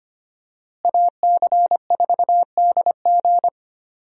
Morse code, heard as a keyed tone, about 25 wpm, 700 Hz. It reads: AC4DG
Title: USA Call Signs - 5 Characters - Repeated 3x - 25wpm